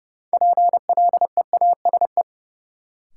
Morse code, heard as a keyed tone, about 32 wpm, 700 Hz. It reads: PLEASE
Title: Sentences from Top 1500 Words - New Word Drill - 30wpm